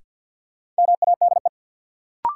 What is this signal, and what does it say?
Morse code, heard as a keyed tone, about 53 wpm, 700 Hz. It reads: MADE
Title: Sentences from Top 200 Words - New Word Drill - 50wpm